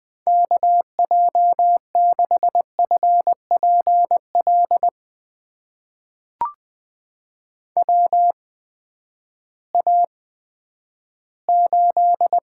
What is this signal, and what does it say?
Morse code, heard as a keyed tone, about 20 wpm, spaced as for 11 wpm, 700 Hz. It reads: KJ6FPL WA8
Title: USA Call Signs - 6 Characters - 8x Character Spacing - 20wpm